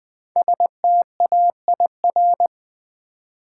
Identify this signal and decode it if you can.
Morse code, heard as a keyed tone, about 20 wpm, 700 Hz. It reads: STAIR
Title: 5 Letter Words - First 3 Letters Given - 20wpm